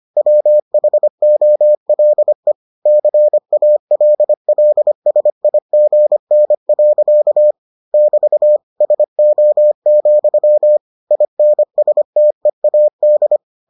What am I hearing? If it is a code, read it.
Morse code, heard as a keyed tone, about 25 wpm, 600 Hz. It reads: WHOLE CALLSIGN. <BT> SO, INSTEAD